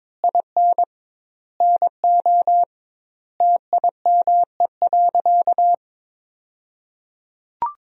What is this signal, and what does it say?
Morse code, heard as a keyed tone, about 22 wpm, 700 Hz. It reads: IN NO TIME.
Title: 2x Word-Spacing - Sentences from Top 400 Words - 22wpm